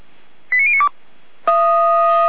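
Five-tone selective call (ZVEI), heard as quick runs of five tones, tones of about 70 ms. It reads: 89092
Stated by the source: Received by SDR, ZVEI five-tone.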